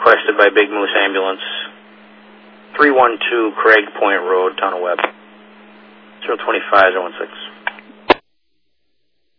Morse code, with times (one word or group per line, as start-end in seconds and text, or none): none